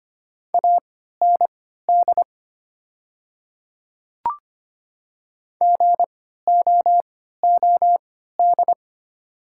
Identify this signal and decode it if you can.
Morse code, heard as a keyed tone, about 25 wpm, spaced as for 14 wpm, 700 Hz. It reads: AND GOOD
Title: 3x Character Spacing - Top 300 Words - 25wpm